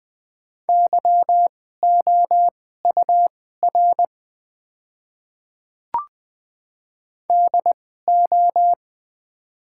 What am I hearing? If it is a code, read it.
Morse code, heard as a keyed tone, about 20 wpm, spaced as for 15 wpm, 700 Hz. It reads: YOUR DO
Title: 2x Character Spacing - Top 100 Words - 20wpm